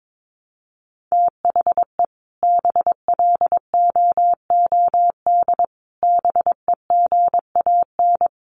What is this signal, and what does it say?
Morse code, heard as a keyed tone, about 22 wpm, 700 Hz. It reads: THE BLOOD BEGAN
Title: Sentences from Top 1200 Words 22wpm